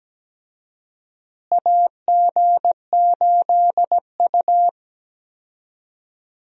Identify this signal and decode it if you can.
Morse code, heard as a keyed tone, about 17 wpm, 700 Hz. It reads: AG8U
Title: USA Call Signs - 4 Characters - Repeated 3x - 17wpm